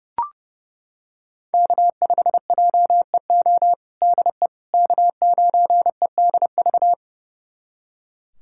Morse code, heard as a keyed tone, about 30 wpm, 700 Hz. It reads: K5JEO DE K9EDV
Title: USA Callsign DE Callsign - 5 Characters - 30wpm